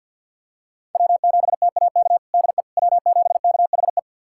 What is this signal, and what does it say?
Morse code, heard as a keyed tone, about 47 wpm, 700 Hz. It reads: W7TAK DE W7KHE
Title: USA Callsign DE Callsign - 5 Characters - 50wpm